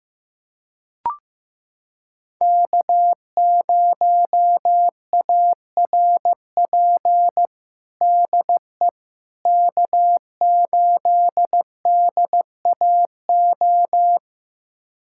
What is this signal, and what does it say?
Morse code, heard as a keyed tone, about 15 wpm, 700 Hz. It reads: K0ARP DE K8DAO
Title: USA Callsign DE Callsign - 5 Characters - 15wpm